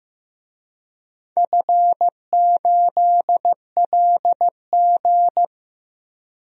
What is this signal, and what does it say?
Morse code, heard as a keyed tone, about 15 wpm, 700 Hz. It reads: F8LG
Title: DXCC-100 Call Signs - 4 Characters - Repeated 2x - 15wpm